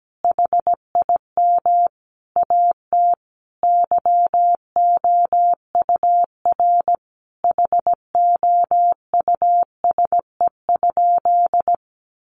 Morse code, heard as a keyed tone, about 17 wpm, 700 Hz. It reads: HIM AT YOUR HOUSE?
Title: Sentences from Top 1000 Words 17wpm